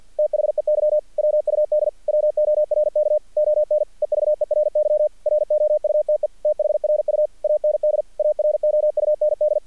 Morse code, heard as a keyed tone, about 48 wpm, 600 Hz. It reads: THEY WUD WORK ON EVERY ROUTE THRU AND AROUND
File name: archived / CRSnet